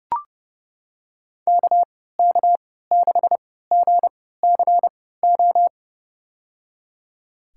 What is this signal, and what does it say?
Morse code, heard as a keyed tone, about 30 wpm, spaced as for 17 wpm, 700 Hz. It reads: KK6GCO
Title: USA Call Signs - 6 Characters - 3x Character Spacing - 30wpm